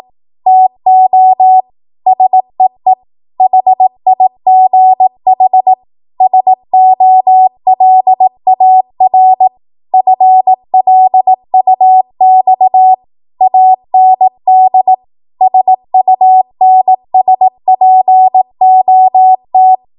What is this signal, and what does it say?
Morse code, heard as a keyed tone, about 18 wpm, 750 Hz. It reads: TO SEE HIGH SOLAR FLUX AND SUNSPOT
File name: Archive / 18 WPM / 18WPM